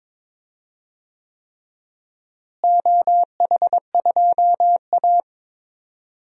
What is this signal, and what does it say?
Morse code, heard as a keyed tone, about 22 wpm, 700 Hz. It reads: OH2A
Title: DXCC-100 Call Signs - 4 Characters - Repeated 3x - 22wpm